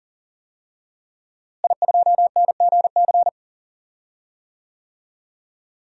Morse code, heard as a keyed tone, about 40 wpm, 700 Hz. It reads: I2NGC
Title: DXCC-100 Call Signs - 5 Characters - Repeated 3x - 40wpm